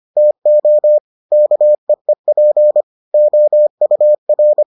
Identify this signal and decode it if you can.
Morse code, heard as a keyed tone, about 25 wpm, 600 Hz. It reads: TO KEEP OUR